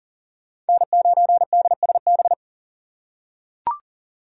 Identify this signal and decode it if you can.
Morse code, heard as a keyed tone, about 38 wpm, 700 Hz. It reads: N9DSB
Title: USA Call Signs - 4x Character Spacing - 40wpm